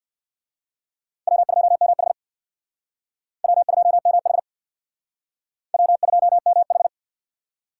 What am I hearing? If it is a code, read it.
Morse code, heard as a keyed tone, about 61 wpm, 700 Hz. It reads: W2MH W2MH W2MH
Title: Speed Racing - USA 1x2 Call Signs 50wpm